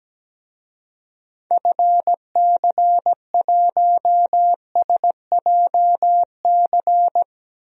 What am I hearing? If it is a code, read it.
Morse code, heard as a keyed tone, about 17 wpm, 700 Hz. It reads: FC1SJC
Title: DXCC-100 Call Signs - 6 Characters - Repeated 3x - 17wpm